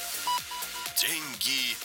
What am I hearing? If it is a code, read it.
Speech. There is no code.